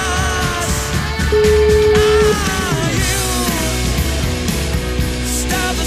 Ringback tone (Europe, 425 Hz), with noise.